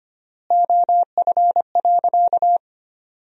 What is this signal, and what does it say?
Morse code, heard as a keyed tone, about 25 wpm, 700 Hz. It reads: OF.
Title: 3x Word-Spacing - Sentences from Top 300 Words - 25wpm